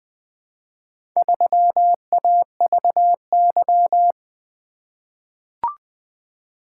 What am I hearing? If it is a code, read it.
Morse code, heard as a keyed tone, about 20 wpm, 700 Hz. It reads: 3AVY
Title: Four Letter-Number Combinations - 20wpm